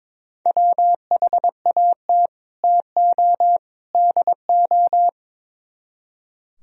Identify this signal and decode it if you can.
Morse code, heard as a keyed tone, about 24 wpm, 700 Hz. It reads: WHAT TO DO
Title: Speed-Racing - Sets of 3 Words 22wpm